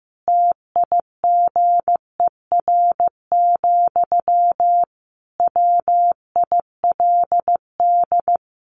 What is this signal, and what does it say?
Morse code, heard as a keyed tone, about 15 wpm, 700 Hz. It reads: TIGER, WILD
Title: A Sound Of Thunder 15wpm